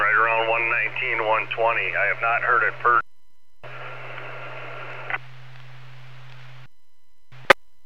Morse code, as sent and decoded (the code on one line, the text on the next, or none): none